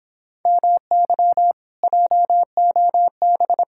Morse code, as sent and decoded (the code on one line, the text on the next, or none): -- -.-- / .--- --- -...
MY JOB